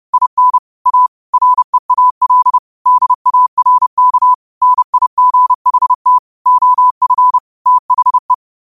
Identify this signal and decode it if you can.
Morse code, heard as a keyed tone, about 30 wpm, 1000 Hz. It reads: IN A REAL DARK NIGHT OF THE